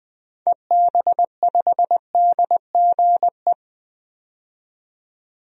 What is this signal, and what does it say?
Morse code, heard as a keyed tone, about 20 wpm, 700 Hz. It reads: EB5DGE
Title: DXCC-100 Call Signs - 6 Characters - 20wpm